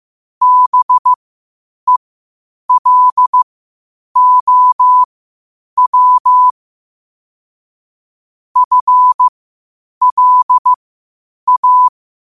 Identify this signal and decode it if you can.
Morse code, heard as a keyed tone, about 15 wpm, spaced as for 8 wpm, 1000 Hz. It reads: BELOW FLA
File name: qotdfiles / 8wpm